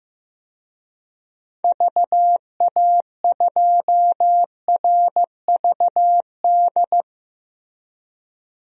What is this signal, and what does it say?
Morse code, heard as a keyed tone, about 15 wpm, 700 Hz. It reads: VA2RVD
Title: DXCC-100 Call Signs - 6 Characters - Repeated 2x - 15wpm